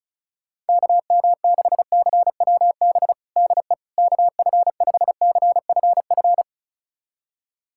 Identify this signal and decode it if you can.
Morse code, heard as a keyed tone, about 36 wpm, 700 Hz. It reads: KM6CWB DE KF5CFF
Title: USA Callsign DE Callsign 35wpm